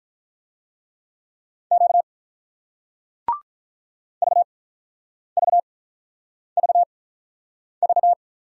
Morse code, heard as a keyed tone, about 48 wpm, spaced as for 9 wpm, 700 Hz. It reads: Q VVVV